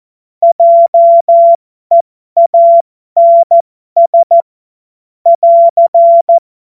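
Morse code, heard as a keyed tone, about 14 wpm, spaced as for 12 wpm, 680 Hz. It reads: JEANS <AR>
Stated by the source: Morse code joke.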